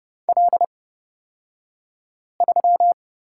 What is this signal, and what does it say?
Morse code, heard as a keyed tone, about 31 wpm, spaced as for 5 wpm, 700 Hz. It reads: L3
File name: Lesson 38d - Letter q - Two Characters 30wpm